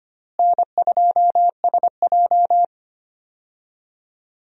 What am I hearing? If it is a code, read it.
Morse code, heard as a keyed tone, about 25 wpm, 700 Hz. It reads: N2SJ
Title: USA Call Signs - 4 Characters - Repeated 2x - 25wpm